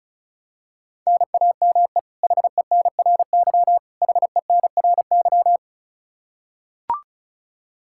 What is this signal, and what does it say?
Morse code, heard as a keyed tone, about 36 wpm, 700 Hz. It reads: NAME HENRY HENRY
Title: Practice QSO - Exchange 1 - Beginning - 35wpm